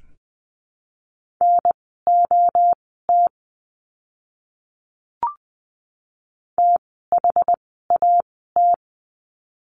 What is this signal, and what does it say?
Morse code, heard as a keyed tone, about 20 wpm, spaced as for 15 wpm, 700 Hz. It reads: NOT THAT
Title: 061 Lesson 11f - Letter h - Words 20wpm